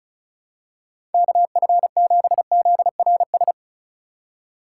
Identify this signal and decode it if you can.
Morse code, heard as a keyed tone, about 36 wpm, 700 Hz. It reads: KF7ZRS
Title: USA Call Signs - 6 Characters - 4x Character Spacing - 35wpm